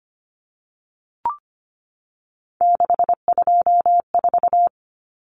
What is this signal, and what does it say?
Morse code, heard as a keyed tone, about 25 wpm, 700 Hz. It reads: 624
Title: Numbers - 3-Digit 25wpm